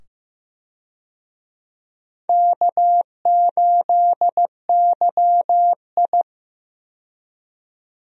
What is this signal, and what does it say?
Morse code, heard as a keyed tone, about 15 wpm, 700 Hz. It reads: K8YI
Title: USA Call Signs - 4 Characters - 15wpm